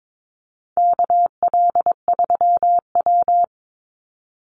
Morse code, heard as a keyed tone, about 22 wpm, 690 Hz. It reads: KL3W
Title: Call Signs Repeated 3x - 22wpm